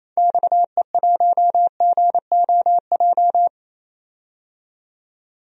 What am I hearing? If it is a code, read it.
Morse code, heard as a keyed tone, about 28 wpm, 700 Hz. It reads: XE1GOJ
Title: DXCC-100 Call Signs - 6 Characters - 28wpm